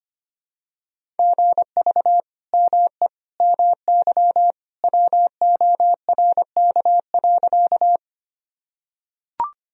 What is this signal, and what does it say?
Morse code, heard as a keyed tone, about 25 wpm, 700 Hz. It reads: GV ME MY WORK.
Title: Sentences from Top 200 Words and Abbreviations 25wpm